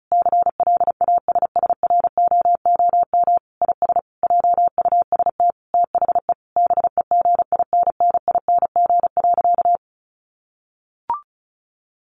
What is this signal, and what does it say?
Morse code, heard as a keyed tone, about 34 wpm, 700 Hz. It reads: CLASSROOM IS JUST THE BEGINNING.
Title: Sentences from Top 1500 Words 35wpm